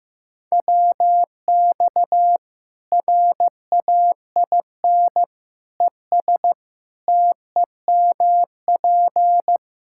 Morse code, heard as a keyed tone, about 15 wpm, 700 Hz. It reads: WX RAIN ES TEMP
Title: Practice QSO - Exchange 2 - Info - 15wpm